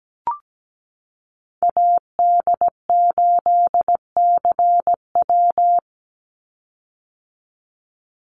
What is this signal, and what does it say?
Morse code, heard as a keyed tone, about 17 wpm, 700 Hz. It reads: AD8CW
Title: Call Signs - 17wpm